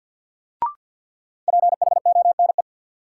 Morse code, heard as a keyed tone, about 50 wpm, 700 Hz. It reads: PHONE
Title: Sentences from Top 1100 Words - New Word Drill - 50wpm